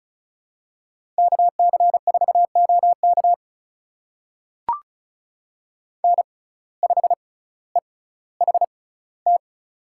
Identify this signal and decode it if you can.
Morse code, heard as a keyed tone, about 36 wpm, spaced as for 18 wpm, 700 Hz. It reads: KC4OK N5EHT
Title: USA Call Signs - 6x Character Spacing - 35wpm